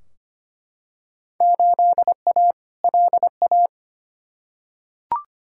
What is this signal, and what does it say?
Morse code, heard as a keyed tone, about 25 wpm, 700 Hz. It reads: 8A LA
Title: Field Day Exchanges Encore - 25wpm